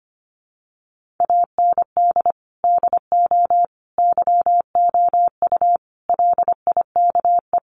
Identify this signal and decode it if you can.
Morse code, heard as a keyed tone, about 25 wpm, 700 Hz. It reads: AND DO YOU LIKE
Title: Sentences from Top 100 Words 25wpm